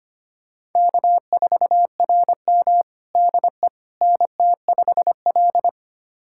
Morse code, heard as a keyed tone, about 25 wpm, 700 Hz. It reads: K4RM DE NT5L
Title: USA Callsign DE Callsign - 4 Characters - 25wpm